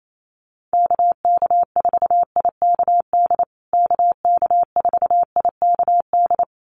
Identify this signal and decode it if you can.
Morse code, heard as a keyed tone, about 28 wpm, 700 Hz. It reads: KK4IKD KK4IKD
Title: USA Call Signs Repeated 2x 28wpm